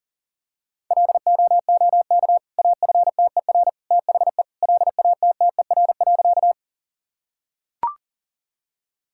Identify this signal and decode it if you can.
Morse code, heard as a keyed tone, about 38 wpm, 700 Hz. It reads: LOOK AFTER THE LATTER.
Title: Sentences from Top 1600 Words 40wpm